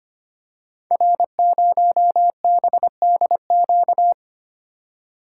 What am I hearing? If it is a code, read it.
Morse code, heard as a keyed tone, about 25 wpm, 700 Hz. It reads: R0BDQ
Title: Code Groups - 5 Letter:Numbers - 25wpm